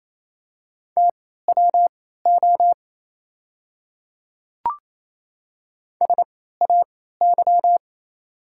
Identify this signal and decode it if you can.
Morse code, heard as a keyed tone, about 27 wpm, spaced as for 16 wpm, 700 Hz. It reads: TWO SAY
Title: 3x Character Spacing - Top 200 Words - 28wpm